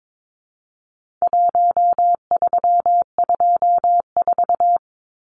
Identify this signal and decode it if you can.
Morse code, heard as a keyed tone, about 22 wpm, 700 Hz. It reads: 1324
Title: Numbers - Contest Style - 4-Digit 22wpm